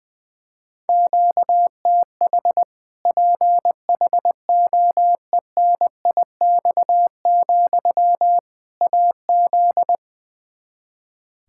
Morse code, heard as a keyed tone, about 20 wpm, 700 Hz. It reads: QTH PHOENIX, AZ
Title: Practice QSO - Exchange 1 - Beginning - 20wpm